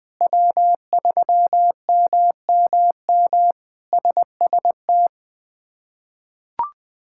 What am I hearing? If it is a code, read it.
Morse code, heard as a keyed tone, about 20 wpm, 700 Hz. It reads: W3MMM SST
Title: K1USN SST QSO Practice - 20wpm